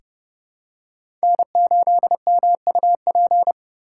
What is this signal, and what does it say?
Morse code, heard as a keyed tone, about 32 wpm, 700 Hz. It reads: N8MUP
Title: USA Call Signs - 5 Characters - 4x Character Spacing - 30wpm